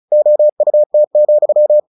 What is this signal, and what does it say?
Morse code, heard as a keyed tone, about 34 wpm, 600 Hz. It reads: OUT,
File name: the old man and the sea / the old man and the sea 1 35wpm